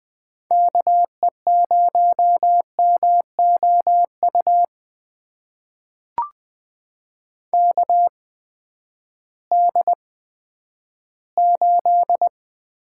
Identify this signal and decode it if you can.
Morse code, heard as a keyed tone, about 20 wpm, spaced as for 11 wpm, 700 Hz. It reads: KE0MOU KD8
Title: USA Call Signs - 6 Characters - 8x Character Spacing - 20wpm